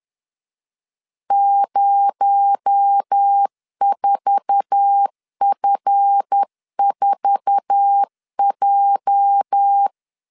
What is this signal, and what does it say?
Morse code, heard as a keyed tone, about 10 wpm, 800 Hz. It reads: 04F4J
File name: LEZIONI / 8 YJ45 / 8 YJ4 10wpm 2633